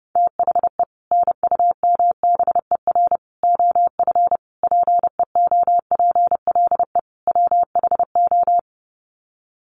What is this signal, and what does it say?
Morse code, heard as a keyed tone, about 28 wpm, 700 Hz. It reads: THE NUMBER OF PEOPLE WHO
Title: Sets of 5 Words Encore 30wpm